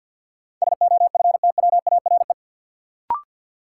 Morse code, heard as a keyed tone, about 53 wpm, 700 Hz. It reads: SOFTWARE
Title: Sentences from Top 1000 Words - Review Part 3 - New Word Drill - 50wpm